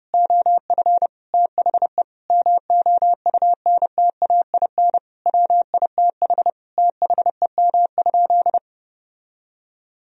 Morse code, heard as a keyed tone, about 30 wpm, 700 Hz. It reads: OF THE MOUNTAIN WITH THEM?
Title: Sentences from Top 1000 Words - Review Part 3 - 30wpm